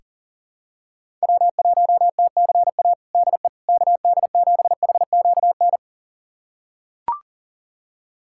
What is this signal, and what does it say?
Morse code, heard as a keyed tone, about 37 wpm, 700 Hz. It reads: W1TCA DE KD7HQN
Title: USA Callsign DE Callsign 40wpm